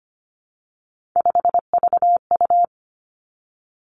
Morse code, heard as a keyed tone, about 25 wpm, 700 Hz. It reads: 5VU
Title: Three Challenging Letter-Number Combinations 25wpm